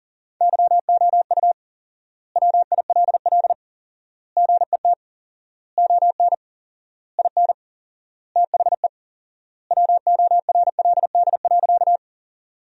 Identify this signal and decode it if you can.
Morse code, heard as a keyed tone, about 40 wpm, 700 Hz. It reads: YOU WILL GET ON IN THE WORLD.
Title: 4x Word-Spacing - Sentences from Top 200 Words - 40wpm